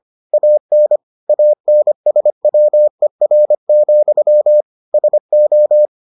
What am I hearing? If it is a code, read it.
Morse code, heard as a keyed tone, about 25 wpm, 600 Hz. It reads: AN ANSWER, SO